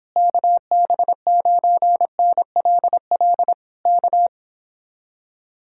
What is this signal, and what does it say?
Morse code, heard as a keyed tone, about 26 wpm, 700 Hz. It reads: KB9NLL K